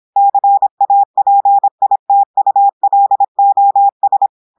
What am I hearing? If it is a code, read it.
Morse code, heard as a keyed tone, about 26 wpm, 800 Hz. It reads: CAPITULOS
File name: audios / portugais / portugais texte en clair 1 vitesse 26 mots minutes